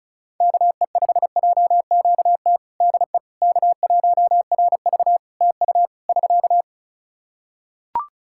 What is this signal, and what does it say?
Morse code, heard as a keyed tone, about 34 wpm, 700 Hz. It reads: KE5JQT DE K1RV TU <SK>
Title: Practice QSO - Exchange 4 - Ending - 35wpm